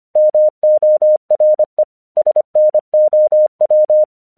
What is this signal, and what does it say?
Morse code, heard as a keyed tone, about 24 wpm, 610 Hz. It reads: MORE SNOW
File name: 1 English 25wpm / AQC18